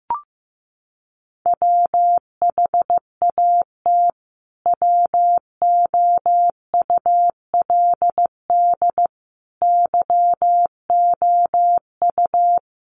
Morse code, heard as a keyed tone, about 15 wpm, 700 Hz. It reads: WHAT WOULD YOU
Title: Sentences from Top 1200 Words 15wpm